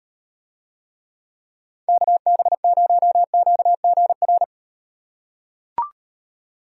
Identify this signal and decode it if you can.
Morse code, heard as a keyed tone, about 38 wpm, 700 Hz. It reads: KB0QGR